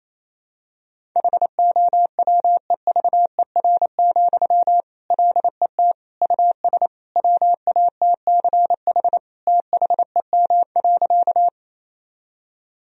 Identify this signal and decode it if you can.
Morse code, heard as a keyed tone, about 28 wpm, 700 Hz. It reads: HOWEVER, LET US WATCH THEM.
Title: Sentences from Top 1000 Words - Review Part 1 - 28wpm